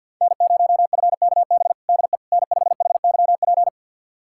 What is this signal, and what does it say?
Morse code, heard as a keyed tone, about 49 wpm, 700 Hz. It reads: N0FKB DE N5HYP
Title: USA Callsign DE Callsign - 5 Characters - 50wpm